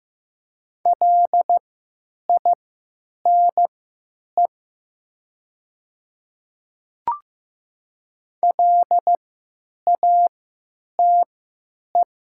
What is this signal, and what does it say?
Morse code, heard as a keyed tone, about 15 wpm, spaced as for 9 wpm, 700 Hz. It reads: LINE LATE
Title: 3x Character Spacing - Top 500 Words - 15wpm